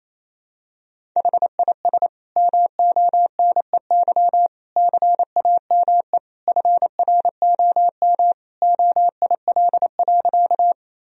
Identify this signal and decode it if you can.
Morse code, heard as a keyed tone, about 27 wpm, 700 Hz. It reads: HIS MONEY CAME FROM OIL.
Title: Sentences from Top 1000 Words - Review Part 4 - 28wpm